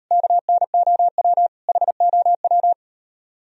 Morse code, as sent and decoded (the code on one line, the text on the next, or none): -.- -. --- .-- / .... --- .--
KNOW HOW